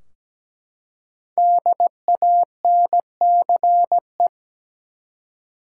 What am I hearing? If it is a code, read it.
Morse code, heard as a keyed tone, about 17 wpm, 700 Hz. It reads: DANCE
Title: Five Letter Words - 17wpm